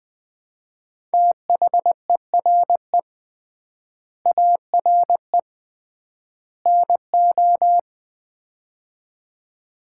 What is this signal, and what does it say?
Morse code, heard as a keyed tone, about 20 wpm, 700 Hz. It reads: THERE ARE NO
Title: 3x Word-Spacing - Sets of 3 Words - 20wpm